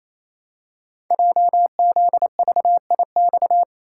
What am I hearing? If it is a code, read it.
Morse code, heard as a keyed tone, about 28 wpm, 700 Hz. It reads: JZVIX
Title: Code Groups - 5 Letters - 28wpm